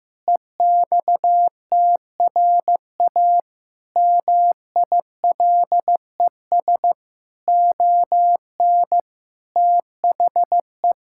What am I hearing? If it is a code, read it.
Morse code, heard as a keyed tone, about 14 wpm, 700 Hz. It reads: EXTRA MILES ON THE
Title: Pangram Challenge v2 - 15wpm